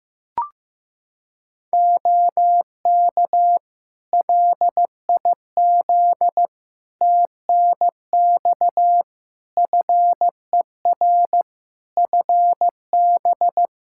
Morse code, heard as a keyed tone, about 15 wpm, 700 Hz. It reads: OK LIZ TNX FER FB
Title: Practice QSO - Exchange 4 - Ending - 15wpm